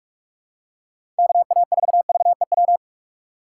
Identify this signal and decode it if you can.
Morse code, heard as a keyed tone, about 43 wpm, 700 Hz. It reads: KA4VEW
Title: USA Call Signs - 6 Characters - 3x Character Spacing - 45wpm